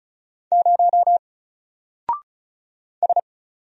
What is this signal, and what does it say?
Morse code, heard as a keyed tone, about 35 wpm, 700 Hz. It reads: 0 S
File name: Single-Letter-Number-35wpm